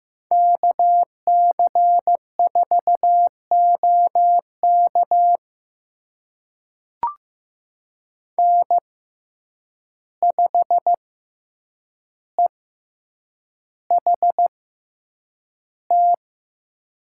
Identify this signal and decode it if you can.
Morse code, heard as a keyed tone, about 15 wpm, spaced as for 8 wpm, 700 Hz. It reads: KC4OK N5EHT
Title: USA Call Signs - 6x Character Spacing - 15wpm